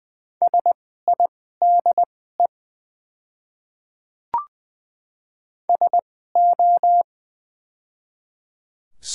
Morse code, heard as a keyed tone, about 20 wpm, spaced as for 14 wpm, 700 Hz. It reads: SIDE SO
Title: 2x Character Spacing - Top 100 Words - 20wpm